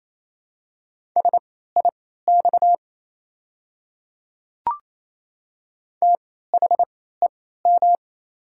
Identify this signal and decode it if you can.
Morse code, heard as a keyed tone, about 28 wpm, spaced as for 16 wpm, 690 Hz. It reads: SIX THEM
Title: 3x Character Spacing - Top 500 Words - 28wpm